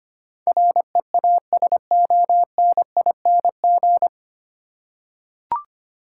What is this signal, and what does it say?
Morse code, heard as a keyed tone, about 25 wpm, 700 Hz. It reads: REASONING
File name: Nine Letter Words - 25wpm